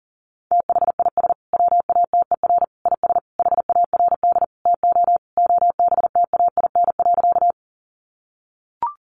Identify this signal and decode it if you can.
Morse code, heard as a keyed tone, about 40 wpm, 700 Hz. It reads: THIS WATER IS HARD TO OBTAIN.
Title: Sentences from Top 1300 Words 40wpm